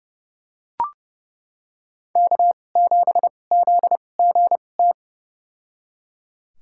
Morse code, heard as a keyed tone, about 32 wpm, spaced as for 22 wpm, 700 Hz. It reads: K7ZGT
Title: USA Call Signs - 5 Characters - 2x Character Spacing - 30wpm